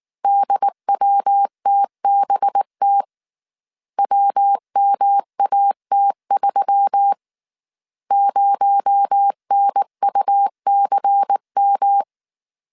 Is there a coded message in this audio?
Yes: Morse code from 0.2 to 12.0 s, heard as a keyed tone, about 19 wpm, 800 Hz. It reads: DWTBT WMAT3 0NUCM